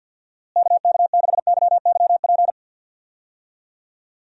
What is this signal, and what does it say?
Morse code, heard as a keyed tone, about 51 wpm, 700 Hz. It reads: KK6YYP
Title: USA Call Signs - 6 Characters - Repeated 2x - 50wpm